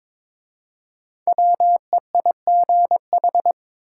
Morse code, heard as a keyed tone, about 22 wpm, 700 Hz. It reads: WEIGH
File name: 5 Letter Words - First 3 Letters Given - 22wpm